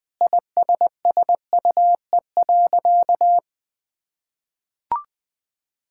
Morse code, heard as a keyed tone, about 21 wpm, 700 Hz. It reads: ISSUE.
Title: Sentences with 6-Word Phrases - 20wpm